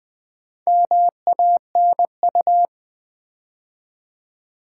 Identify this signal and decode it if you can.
Morse code, heard as a keyed tone, about 20 wpm, 700 Hz. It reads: MANU
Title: Build A Word - 4 Letters At A Time - 11-Letter Words - 20wpm